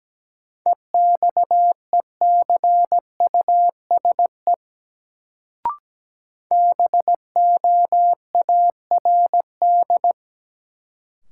Morse code, heard as a keyed tone, about 17 wpm, 700 Hz. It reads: EXECUSE BOARD
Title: Sentences from Top 700 Words - New Word Drill - 17wpm